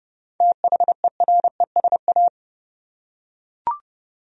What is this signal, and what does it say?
Morse code, heard as a keyed tone, about 29 wpm, 700 Hz. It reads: THERESA
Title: USA Common Names - 30wpm